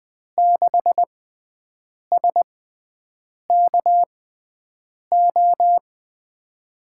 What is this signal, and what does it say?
Morse code, heard as a keyed tone, about 20 wpm, spaced as for 7 wpm, 700 Hz. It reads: 6SKO
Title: USA Call Signs - 6 Characters - 6x Character Spacing - 20wpm